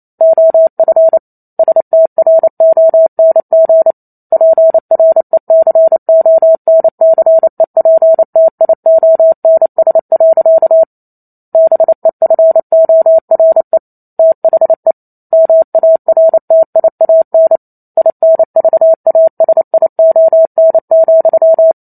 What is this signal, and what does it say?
Morse code, heard as a keyed tone, about 28 wpm, 650 Hz. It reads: OF STRONG PRECONCEPTIONS. BEFORE THE MARTIAN INVASION,
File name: Chapter 2 - 28 WPM